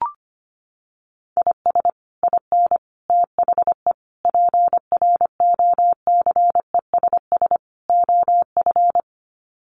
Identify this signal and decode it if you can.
Morse code, heard as a keyed tone, about 25 wpm, 700 Hz. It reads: IS IN THE PROCESS OF
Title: Sets of 5 Words Encore2 25wpm